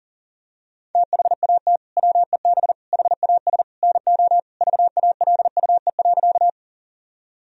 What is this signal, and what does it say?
Morse code, heard as a keyed tone, about 37 wpm, 700 Hz. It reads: THAT WEB HAS NO VALUE.
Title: Sentences from Top 1000 Words - Review Part 4 - 40wpm